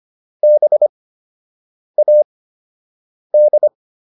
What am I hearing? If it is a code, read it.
Morse code, heard as a keyed tone, about 25 wpm, spaced as for 7 wpm, 600 Hz. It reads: BAD